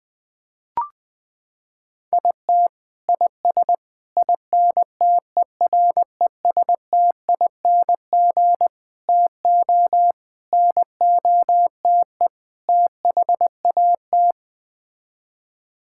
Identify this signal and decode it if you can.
Morse code, heard as a keyed tone, about 20 wpm, 700 Hz. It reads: IT IS INTERESTING TO NOTE THAT
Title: Sets of 6 Words - 20wpm